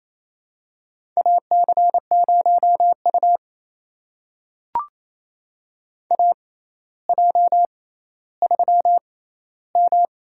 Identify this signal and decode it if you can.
Morse code, heard as a keyed tone, about 28 wpm, spaced as for 14 wpm, 700 Hz. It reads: AC0U AJ3M
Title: USA Call Signs - 4 Characters - 6x Character Spacing - 28wpm